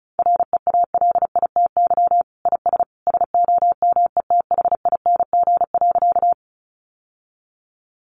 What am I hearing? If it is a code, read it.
Morse code, heard as a keyed tone, about 34 wpm, 700 Hz. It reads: REALITY IS SOMETHING.
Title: Sentences from Top 1200 Words 35wpm